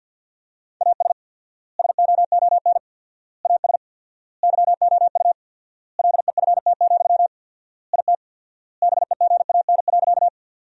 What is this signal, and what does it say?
Morse code, heard as a keyed tone, about 50 wpm, 700 Hz. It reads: AS SOON AS YOU LEFT, IT BEGAN.
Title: 4x Word-Spacing - Sentences from Top 400 Words - 50wpm